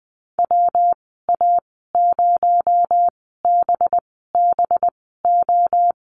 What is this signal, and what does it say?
Morse code, heard as a keyed tone, about 20 wpm, spaced as for 14 wpm, 700 Hz. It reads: WA0BBO
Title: 238 Lesson 38g - Letter q - Call Signs 20wpm